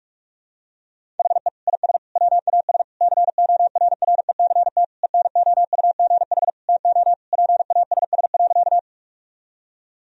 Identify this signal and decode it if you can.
Morse code, heard as a keyed tone, about 41 wpm, 700 Hz. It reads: HE IS WAS CORRECT ENOUGH TO PASS.
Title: Sentences from Top 1000 Words - Review Part 2 - 45wpm